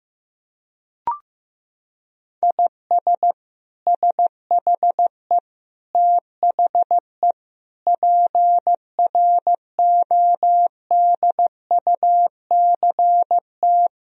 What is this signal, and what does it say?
Morse code, heard as a keyed tone, about 15 wpm, 700 Hz. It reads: IS SHE THE PRODUCT
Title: Sentences from Top 1000 Words - Review Part 2 - 15wpm